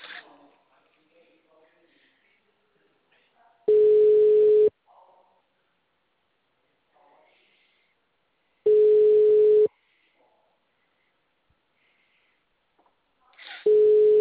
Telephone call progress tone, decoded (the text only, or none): ringback tone